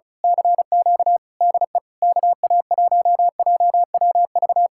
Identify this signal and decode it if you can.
Morse code, heard as a keyed tone, about 32 wpm, 700 Hz. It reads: CQ DE KA1JWV
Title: Speed-Racing Common QSO Phrases - 35wpm